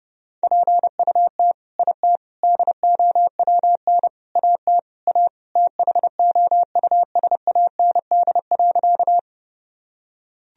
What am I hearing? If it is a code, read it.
Morse code, heard as a keyed tone, about 30 wpm, 700 Hz. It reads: PUT IT DOWN AT A THOUSAND.
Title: Sentences from Top 1000 Words - Review Part 3 - 30wpm